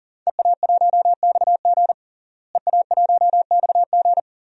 Morse code, heard as a keyed tone, about 37 wpm, 690 Hz. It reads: EA1XG EA1XG
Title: DXCC-100 Call Signs - 5 Characters - Repeated 2x - 40wpm